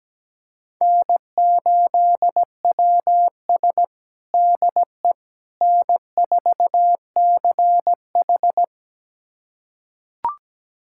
Morse code, heard as a keyed tone, about 17 wpm, 700 Hz. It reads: N8WS DE N4CH
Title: USA Callsign DE Callsign - 4 Characters - 17wpm